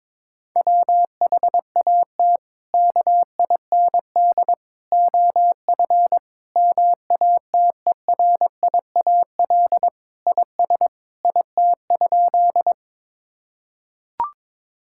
Morse code, heard as a keyed tone, about 23 wpm, 700 Hz. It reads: WHAT KIND OF MATERIAL IS IT?
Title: Sentences from Top 1000 Words - Review Part 5 - 22wpm